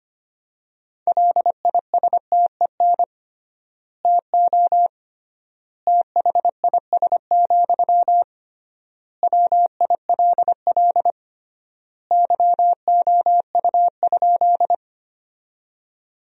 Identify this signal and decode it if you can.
Morse code, heard as a keyed tone, about 25 wpm, 700 Hz. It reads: LISTEN TO THIS, WILL YOU?
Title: 3x Word-Spacing - Sentences from Top 400 Words - 25wpm